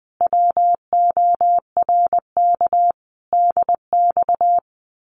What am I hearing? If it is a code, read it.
Morse code, heard as a keyed tone, about 21 wpm, 700 Hz. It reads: WORK DX
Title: Common QSO Phrases Encore 20wpm